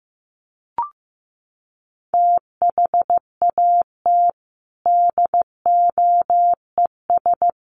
Morse code, heard as a keyed tone, about 15 wpm, 700 Hz. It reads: THAT DOES
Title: Sets of 3 Words 15wpm